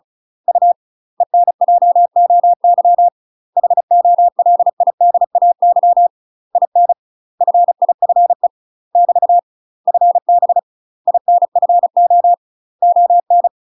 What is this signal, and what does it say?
Morse code, heard as a keyed tone, about 36 wpm, 700 Hz. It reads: U ENJOY HOLIDAY IN FIFE <BT> FB INFO ON
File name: UK MASTER QSOs TRAINING / 35WPM / UK MASTER QSOs 0011